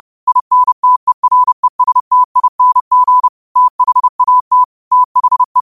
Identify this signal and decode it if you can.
Morse code, heard as a keyed tone, about 30 wpm, 1000 Hz. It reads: INTERESTING THAT THE
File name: qotdfiles / 30wpm